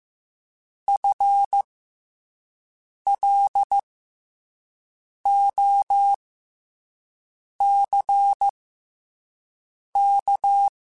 Morse code, heard as a keyed tone, about 15 wpm, spaced as for 5 wpm, 800 Hz. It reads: FLOCK
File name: FL Runs / FL Runs mp3s / FL Runs @5